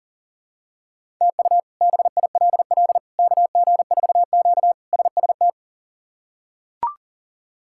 Morse code, heard as a keyed tone, about 37 wpm, 700 Hz. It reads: TU BILL KG4Q SST
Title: K1USN SST QSO Practice - 40wpm